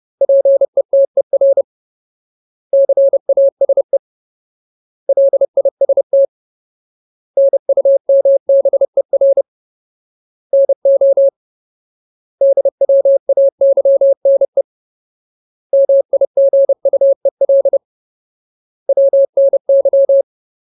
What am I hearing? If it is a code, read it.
Morse code, heard as a keyed tone, about 30 wpm, 550 Hz. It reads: PETER CASE LIST NUMBER NO DWAYNE MIGUEL WNY
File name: Ultimate Word List / 30WPM Ultimate